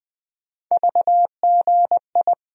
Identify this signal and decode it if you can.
Morse code, heard as a keyed tone, about 20 wpm, 700 Hz. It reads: VGI
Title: USA Call Sign Postfixes - 3 Letters 20wpm